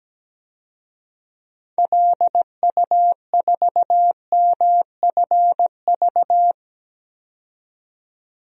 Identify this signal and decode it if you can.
Morse code, heard as a keyed tone, about 17 wpm, 700 Hz. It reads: LU4MFV
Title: DXCC-100 Call Signs - 6 Characters - 17wpm